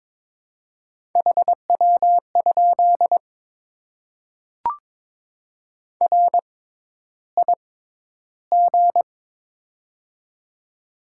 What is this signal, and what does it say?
Morse code, heard as a keyed tone, about 22 wpm, spaced as for 11 wpm, 700 Hz. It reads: HW? RIG
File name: Top 20 QSO Elements - 6x Character Spacing - 22wpm